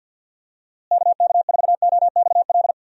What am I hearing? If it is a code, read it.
Morse code, heard as a keyed tone, about 49 wpm, 700 Hz. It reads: KK4OXL
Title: USA Call Signs - 6 Characters - Repeated 2x - 50wpm